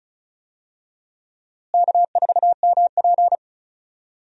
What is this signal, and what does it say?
Morse code, heard as a keyed tone, about 34 wpm, 700 Hz. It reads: K4MP
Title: USA Call Signs - 4 Characters - Repeated 3x - 35wpm